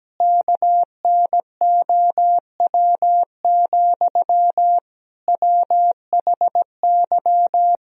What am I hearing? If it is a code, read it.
Morse code, heard as a keyed tone, about 17 wpm, 700 Hz. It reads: KNOW, WHY
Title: Sentences from Top 300 Words and Abbreviations 17wpm